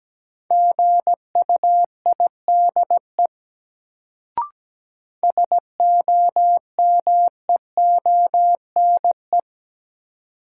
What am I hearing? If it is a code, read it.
Morse code, heard as a keyed tone, about 17 wpm, 700 Hz. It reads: GUIDE SOMEONE
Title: Sentences from Top 600 Words - New Word Drill - 17wpm